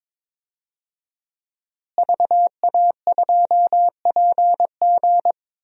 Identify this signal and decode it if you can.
Morse code, heard as a keyed tone, about 22 wpm, 700 Hz. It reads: VA2PG
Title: DXCC-100 Call Signs - 5 Characters - Repeated 3x - 22wpm